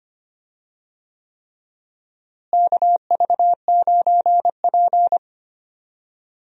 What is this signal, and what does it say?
Morse code, heard as a keyed tone, about 25 wpm, 700 Hz. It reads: KV9P
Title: USA Call Signs - 4 Characters - Repeated 3x - 25wpm